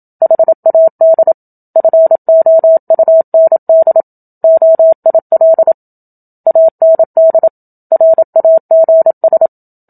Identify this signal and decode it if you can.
Morse code, heard as a keyed tone, about 27 wpm, 650 Hz. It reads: HAD FOUND OIL AND RAGS
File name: Book Two / 01-Chapter 1 - 27 WPM